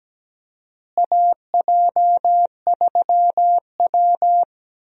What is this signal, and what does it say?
Morse code, heard as a keyed tone, about 17 wpm, 700 Hz. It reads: AJ3W
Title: USA Call Signs - 4 Characters - Repeated 2x - 17wpm